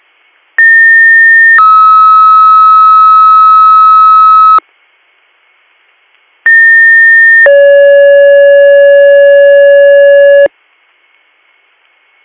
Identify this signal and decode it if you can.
Two-tone page, heard as a one-second tone, then a three-second tone, about 1800 then 1290 Hz, about 1800 then 560 Hz. It is heard from 0.6 to 10.5 s.
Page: bradford-pa